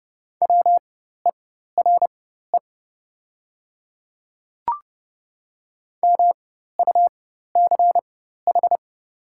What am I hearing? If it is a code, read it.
Morse code, heard as a keyed tone, about 30 wpm, spaced as for 14 wpm, 690 Hz. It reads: WERE MUCH
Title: 4x Character Spacing - Top 200 Words - 30wpm